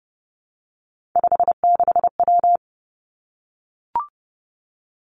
Morse code, heard as a keyed tone, about 30 wpm, 700 Hz. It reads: 56W
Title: Three Challenging Letter-Number Combinations 30wpm